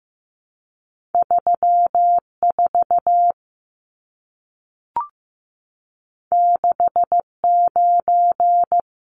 Morse code, heard as a keyed tone, about 15 wpm, 700 Hz. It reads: 34 69
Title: Numbers - 2-Digit 15wpm